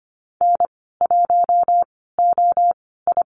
Morse code, heard as a keyed tone, about 25 wpm, spaced as for 16 wpm, 700 Hz. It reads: N1OI
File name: Lesson 15g - Number 5 - Call Signs 25wpm